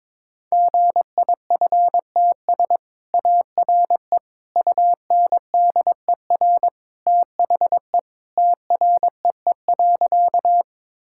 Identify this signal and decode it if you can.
Morse code, heard as a keyed tone, about 22 wpm, 700 Hz. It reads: GIFTS ARE UNDER THE TREE.
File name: practice / 22wpm / Easy Sentences with Hints - 22wpm